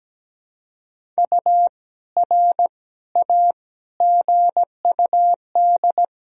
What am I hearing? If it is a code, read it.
Morse code, heard as a keyed tone, about 17 wpm, 700 Hz. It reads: U R A GUD
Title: Sentences from Top 200 Words and Abbreviations 17wpm